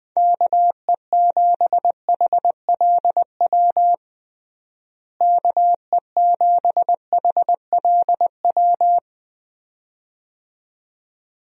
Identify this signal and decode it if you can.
Morse code, heard as a keyed tone, about 20 wpm, 700 Hz. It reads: KE7HLW KE7HLW
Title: USA Call Signs - 6 Characters - Repeated 2x - 20wpm